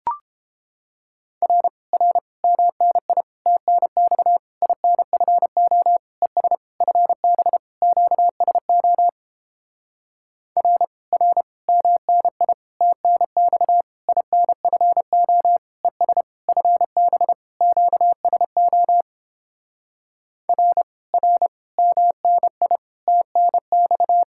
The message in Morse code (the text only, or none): R R MNI TNX INFO ES FB QSO R R MNI TNX INFO ES FB QSO R R MNI TNX